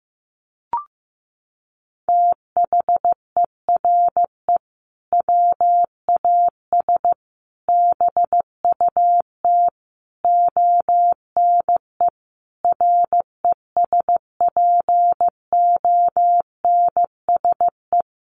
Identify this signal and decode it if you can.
Morse code, heard as a keyed tone, about 15 wpm, 700 Hz. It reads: THERE WAS BUT ONE RESPONSE
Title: Sentences from Top 1400 Words 15wpm